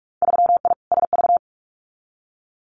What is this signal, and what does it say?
Morse code, heard as a keyed tone, about 46 wpm, 690 Hz. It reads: 3I SV
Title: Winter Field Day Exchanges Encore 45wpm